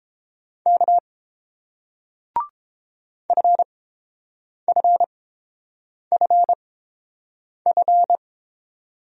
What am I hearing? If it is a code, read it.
Morse code, heard as a keyed tone, about 31 wpm, spaced as for 7 wpm, 700 Hz. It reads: K FFFF